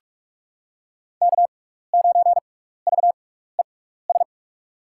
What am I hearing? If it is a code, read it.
Morse code, heard as a keyed tone, about 42 wpm, spaced as for 15 wpm, 700 Hz. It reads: K9VES
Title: USA Call Signs - 6x Character Spacing - 45wpm